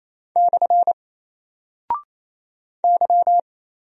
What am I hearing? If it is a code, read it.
Morse code, heard as a keyed tone, about 28 wpm, 700 Hz. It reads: / Y
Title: Single Character (Letter:Number:Punctuation:Prosign) 28wpm